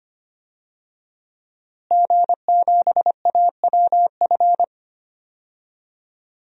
Morse code, heard as a keyed tone, about 25 wpm, 700 Hz. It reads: G7AWF
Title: DXCC-100 Call Signs - 5 Characters - Repeated 2x - 25wpm